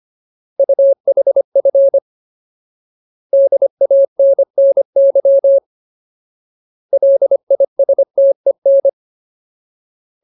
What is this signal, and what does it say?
Morse code, heard as a keyed tone, about 25 wpm, 550 Hz. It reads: UHF DANNY LISTEN
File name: Ultimate Word List / 25WPM Ultimate